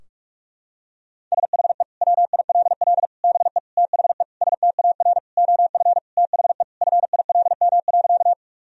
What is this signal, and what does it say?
Morse code, heard as a keyed tone, about 46 wpm, 700 Hz. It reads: SHE WILL BE THE STAR OF THE FILM.
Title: Sentences from Top 1000 Words - Review Part 3 - 45wpm